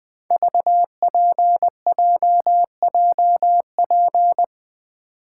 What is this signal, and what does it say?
Morse code, heard as a keyed tone, about 21 wpm, 700 Hz. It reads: VPJJP
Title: Code Groups - 5 Letters - 20wpm